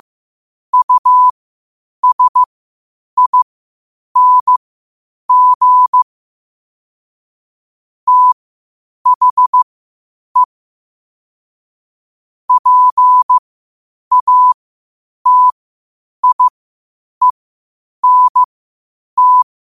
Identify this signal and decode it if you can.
Morse code, heard as a keyed tone, about 15 wpm, spaced as for 8 wpm, 1000 Hz. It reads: USING THE PATIENT
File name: qotdfiles / 8wpm